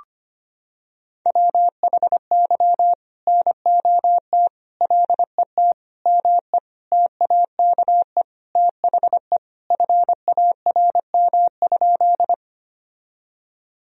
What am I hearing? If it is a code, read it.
Morse code, heard as a keyed tone, about 25 wpm, 700 Hz. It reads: WHY NOT LET ME TAKE THE FARM?
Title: Sentences from Top 1000 Words - Review Part 2 - 25wpm